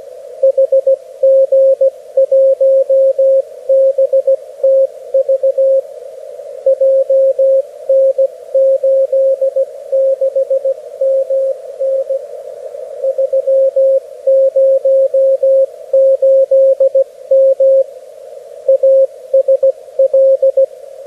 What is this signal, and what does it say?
Morse code, heard as a keyed tone, about 16 wpm, 540 Hz, with radio noise. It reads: HG1BTV JN86MN 308M ASL
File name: beacons / HG1BTV